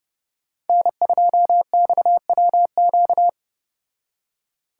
Morse code, heard as a keyed tone, about 29 wpm, 700 Hz. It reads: N2XWQ